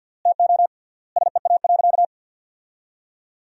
Morse code, heard as a keyed tone, about 50 wpm, 700 Hz. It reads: TO SEA.
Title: 3x Word-Spacing - Sentences from Top 400 Words - 50wpm